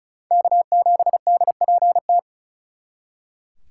Morse code, heard as a keyed tone, about 46 wpm, 700 Hz. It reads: K7DPT